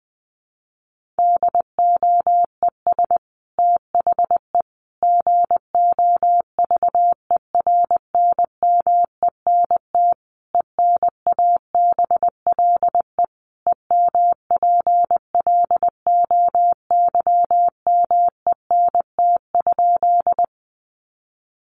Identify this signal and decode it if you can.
Morse code, heard as a keyed tone, about 20 wpm, 700 Hz. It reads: DOES THE GOVERNMENT ENABLE EMPLOYMENT?
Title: Sentences from Top 1500 Words 20wpm